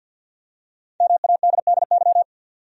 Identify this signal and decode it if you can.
Morse code, heard as a keyed tone, about 47 wpm, 700 Hz. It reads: MADDY
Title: USA Common Names - Max 5 Letters - 50wpm